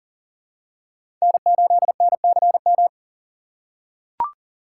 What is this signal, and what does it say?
Morse code, heard as a keyed tone, about 39 wpm, 700 Hz. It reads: N8NCM
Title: USA Call Signs - 5 Characters - 2x Character Spacing - 40wpm